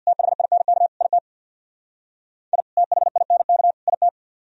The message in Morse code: THINK IT I THINK IT

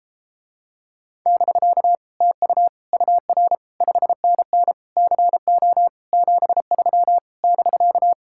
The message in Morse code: <BK> TU UR 5NN CO 73 <BK>

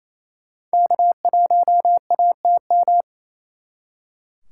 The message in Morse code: K1ATM

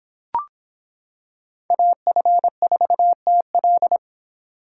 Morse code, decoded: AF4TL

